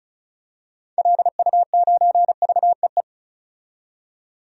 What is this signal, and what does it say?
Morse code, heard as a keyed tone, about 34 wpm, 700 Hz. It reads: LU9VEE
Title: DXCC-100 Call Signs - 6 Characters - 35wpm